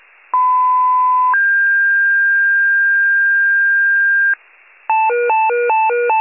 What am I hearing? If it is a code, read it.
Two-tone page, heard as a one-second tone, then a three-second tone, about 1010 then 1600 Hz. It is heard from 0.3 to 4.3 s.